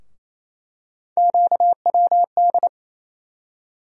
Morse code, heard as a keyed tone, about 28 wpm, 700 Hz. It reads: QWD
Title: USA Call Sign Postfixes - 3 Letters 28wpm